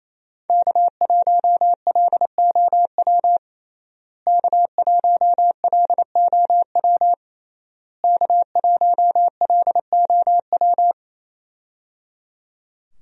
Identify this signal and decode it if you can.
Morse code, heard as a keyed tone, about 28 wpm, 700 Hz. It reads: K1LOW K1LOW K1LOW
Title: USA Call Signs - 5 Characters - Repeated 3x - 28wpm